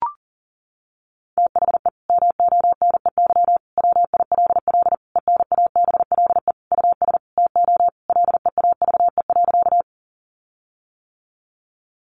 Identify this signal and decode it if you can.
Morse code, heard as a keyed tone, about 40 wpm, 700 Hz. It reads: THE MONEY WILL ENABLE US TO LEAVE.
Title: Sentences from Top 1500 Words 40wpm